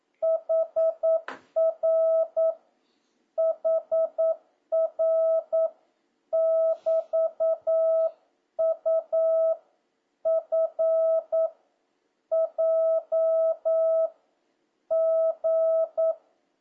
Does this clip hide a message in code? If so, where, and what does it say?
Yes: Morse code from 0.2 to 16.1 s, heard as a keyed tone, about 9 wpm, 650 Hz. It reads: HR HR <BT>U F J G